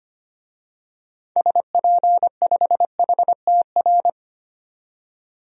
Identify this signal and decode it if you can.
Morse code, heard as a keyed tone, about 25 wpm, 700 Hz. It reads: SP5HTR
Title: DXCC-100 Call Signs - 6 Characters - 25wpm